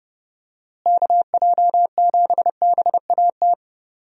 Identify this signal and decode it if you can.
Morse code, heard as a keyed tone, about 30 wpm, 700 Hz. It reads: KJ7BAT
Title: USA Call Signs - 6 Characters - Repeated 3x - 30wpm